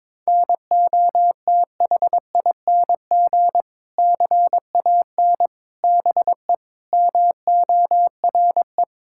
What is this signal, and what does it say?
Morse code, heard as a keyed tone, about 22 wpm, 700 Hz. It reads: NOTHING CAN BE MORE
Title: Sentences from Top 1600 Words 22wpm